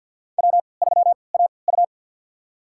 Morse code, heard as a keyed tone, about 53 wpm, spaced as for 29 wpm, 700 Hz. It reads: W3AU
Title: USA Call Signs - 3x Character Spacing - 50wpm